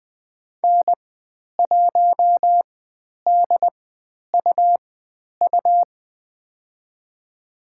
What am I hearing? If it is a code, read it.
Morse code, heard as a keyed tone, about 20 wpm, spaced as for 10 wpm, 700 Hz. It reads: N1DUU
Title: 102 Lesson 17g - Letter u - Call Signs 20wpm